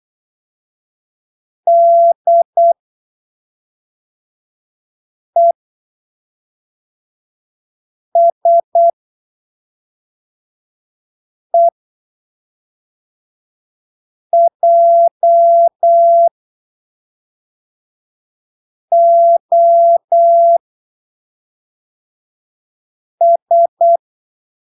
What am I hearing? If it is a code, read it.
Morse code, heard as a keyed tone, about 8 wpm, spaced as for 3 wpm, 680 Hz. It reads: DESEJOS
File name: arquivos / texto / texto 23 08ppm